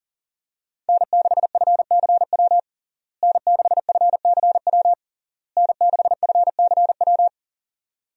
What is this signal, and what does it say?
Morse code, heard as a keyed tone, about 40 wpm, 700 Hz. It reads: N6FCW N6FCW N6FCW
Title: USA Call Signs - 5 Characters - Repeated 3x - 40wpm